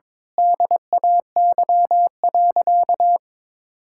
Morse code, heard as a keyed tone, about 22 wpm, 700 Hz. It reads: DAY.
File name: Difficult Sentences with Hints - 22wpm